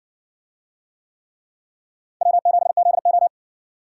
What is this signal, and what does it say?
Morse code, heard as a keyed tone, about 60 wpm, 700 Hz. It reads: W7ZO